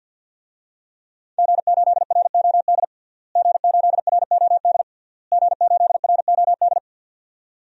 Morse code, heard as a keyed tone, about 52 wpm, 700 Hz. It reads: G8ROD G8ROD G8ROD
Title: DXCC-100 Call Signs - 5 Characters - Repeated 3x - 50wpm